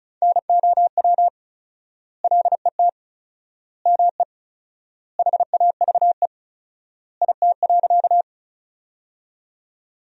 Morse code, heard as a keyed tone, about 36 wpm, 700 Hz. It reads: NOW LET ME HAVE IT.
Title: 4x Word-Spacing - Sentences from Top 300 Words - 35wpm